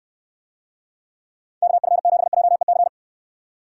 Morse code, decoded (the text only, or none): KF7JEC